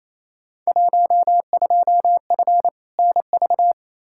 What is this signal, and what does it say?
Morse code, heard as a keyed tone, about 28 wpm, 700 Hz. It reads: 12F NV
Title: Field Day Exchanges Encore - 28wpm